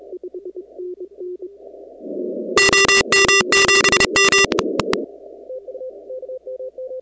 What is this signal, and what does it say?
Morse code, heard as a keyed tone, about 31 wpm, 2750 Hz, with noise. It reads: OM7M
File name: contests / 2019-cw-wpx-kt5j-k5pi- -2330 0-17